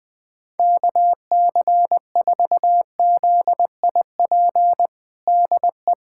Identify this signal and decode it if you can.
Morse code, heard as a keyed tone, about 20 wpm, 700 Hz. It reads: KC4ZIP DE
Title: USA Callsign DE Callsign 20wpm